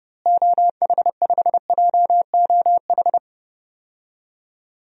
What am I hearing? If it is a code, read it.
Morse code, heard as a keyed tone, about 29 wpm, 700 Hz. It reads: OH5JOH
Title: DXCC-100 Call Signs - 6 Characters - Repeated 2x - 30wpm